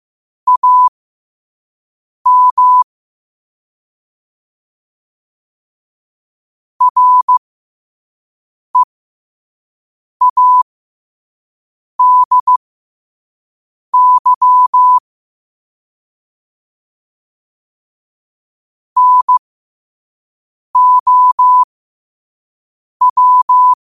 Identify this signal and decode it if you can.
Morse code, heard as a keyed tone, about 15 wpm, spaced as for 5 wpm, 1000 Hz. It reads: AM READY NOW
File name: qotdfiles / 5wpm